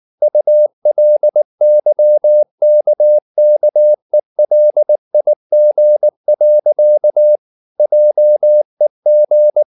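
Morse code, heard as a keyed tone, about 19 wpm, 600 Hz. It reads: ULYKKELIG. JEG
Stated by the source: Slow (QRS) Morse practice at 18 wpm, CW med Gnister.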